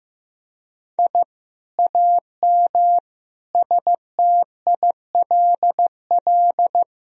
Morse code, heard as a keyed tone, about 16 wpm, 700 Hz. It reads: I AM STILL
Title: Sentences from Top 1000 Words - Review Part 4 - 15wpm